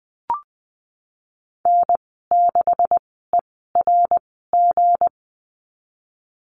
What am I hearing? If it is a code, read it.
Morse code, heard as a keyed tone, about 20 wpm, spaced as for 15 wpm, 700 Hz. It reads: N6ERG
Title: 173 Lesson 28g - Number 9 - Call Signs 20wpm